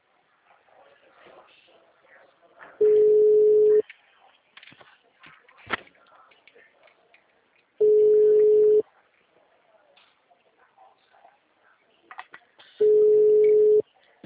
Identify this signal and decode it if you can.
Ringback tone (Europe, 425 Hz).